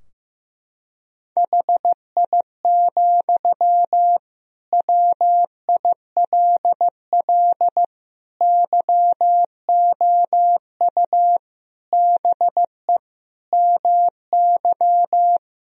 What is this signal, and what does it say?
Morse code, heard as a keyed tone, about 16 wpm, 700 Hz. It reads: HI, WILL YOU BE MY
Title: Sentences from Top 1000 Words - Review Part 3 - 15wpm